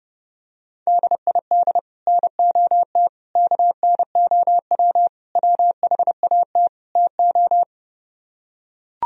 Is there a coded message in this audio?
Yes: Morse code from 0.9 to 7.6 s, heard as a keyed tone, about 30 wpm, 700 Hz. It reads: DID NOT KNOW WHAT TO